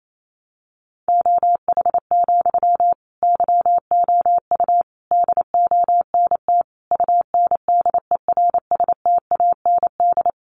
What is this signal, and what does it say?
Morse code, heard as a keyed tone, about 28 wpm, 700 Hz. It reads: OH, YOU DONT UNDERSTAND
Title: Sentences from Top 1000 Words 28wpm